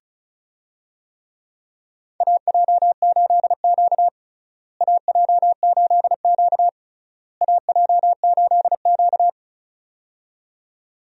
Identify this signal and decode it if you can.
Morse code, heard as a keyed tone, about 34 wpm, 700 Hz. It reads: AJ8Q AJ8Q AJ8Q
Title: USA Call Signs - 4 Characters - Repeated 3x - 35wpm